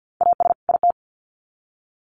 Morse code, heard as a keyed tone, about 50 wpm, 700 Hz. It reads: AS IT
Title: Sets of 2 Words 50wpm